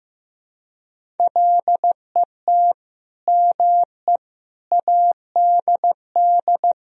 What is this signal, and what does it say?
Morse code, heard as a keyed tone, about 15 wpm, 700 Hz. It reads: LET ME ADD
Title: Sentences from Top 1000 Words - Review Part 1 - 15wpm